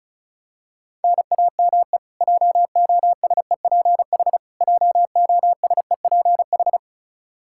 Morse code, heard as a keyed tone, about 34 wpm, 700 Hz. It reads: NAME JOSEPH JOSEPH
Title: K1USN SST QSO Practice 35wpm